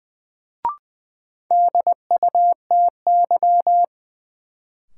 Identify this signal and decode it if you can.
Morse code, heard as a keyed tone, about 20 wpm, 700 Hz. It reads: DUTY
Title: Sentences from Top 1500 Words - New Word Drill Alt - 20wpm